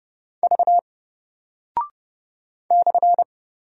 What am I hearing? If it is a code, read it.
Morse code, heard as a keyed tone, about 30 wpm, 700 Hz. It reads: V /
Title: Single Character (Letter:Number:Punctuation:Prosign) 30wpm